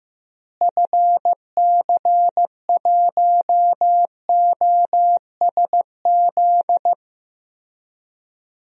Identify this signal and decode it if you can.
Morse code, heard as a keyed tone, about 15 wpm, 700 Hz. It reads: FC1OSZ
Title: DXCC-100 Call Signs - 6 Characters - Repeated 3x - 15wpm